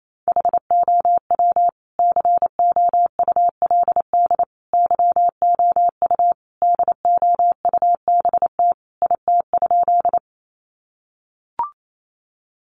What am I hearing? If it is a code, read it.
Morse code, heard as a keyed tone, about 28 wpm, 700 Hz. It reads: HOW COULD YOU DOUBT IT?
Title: Sentences from Top 1500 Words 28wpm